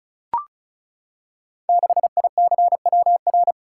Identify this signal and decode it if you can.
Morse code, heard as a keyed tone, about 36 wpm, 700 Hz. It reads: 6ICWR
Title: Code Groups - 5 Letter:Numbers - 35wpm